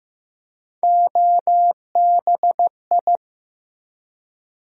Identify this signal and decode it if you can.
Morse code, heard as a keyed tone, about 15 wpm, 700 Hz. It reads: OBI
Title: USA Call Sign Postfixes - 3 Letters 15wpm